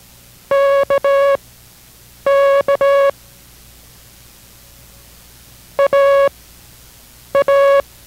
Morse code, heard as a keyed tone, about 18 wpm, spaced as for 7 wpm, 560 Hz. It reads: KK AA